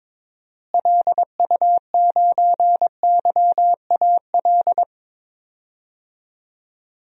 Morse code, heard as a keyed tone, about 22 wpm, 700 Hz. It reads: LU9YAL
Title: DXCC-100 Call Signs - 6 Characters - Repeated 3x - 22wpm